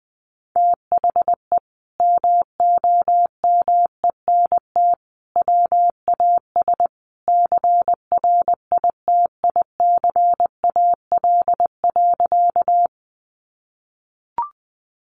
Morse code, heard as a keyed tone, about 20 wpm, 700 Hz. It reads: THE MOMENT WAS CRITICAL.
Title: Sentences from Top 1100 Words 20wpm